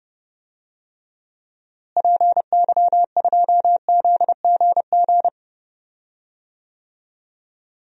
Morse code, heard as a keyed tone, about 30 wpm, 700 Hz. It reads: PY2ZGG
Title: DXCC-100 Call Signs - 6 Characters - Repeated 2x - 30wpm